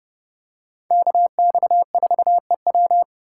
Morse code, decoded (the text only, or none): KX4EW